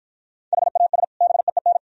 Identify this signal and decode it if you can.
Morse code, heard as a keyed tone, about 48 wpm, 700 Hz. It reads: HAS BEEN